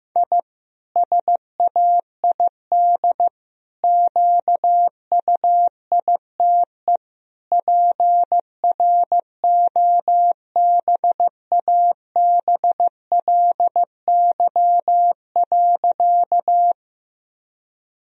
Morse code, heard as a keyed tone, about 15 wpm, 700 Hz. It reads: I SAID QUITE PROBABLY.
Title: Sentences from Top 1000 Words - Review Part 4 - 15wpm